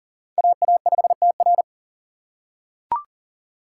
Morse code, heard as a keyed tone, about 40 wpm, 700 Hz. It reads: AA5TR
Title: USA Call Signs - 5 Characters - 3x Character Spacing - 40wpm